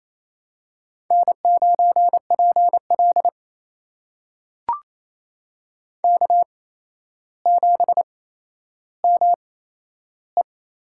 Morse code, heard as a keyed tone, about 28 wpm, spaced as for 12 wpm, 700 Hz. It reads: N9PL K7ME
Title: USA Call Signs - 4 Characters - 8x Character Spacing - 28wpm